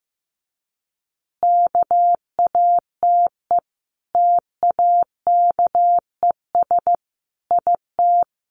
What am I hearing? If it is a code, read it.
Morse code, heard as a keyed tone, about 15 wpm, 700 Hz. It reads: KATE TAKES IT
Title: Easy Phrases 15wpm